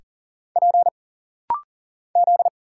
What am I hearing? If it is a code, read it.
Morse code, heard as a keyed tone, about 40 wpm, 700 Hz. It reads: P Z
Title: Single Letters - Rapid-Fire 40wpm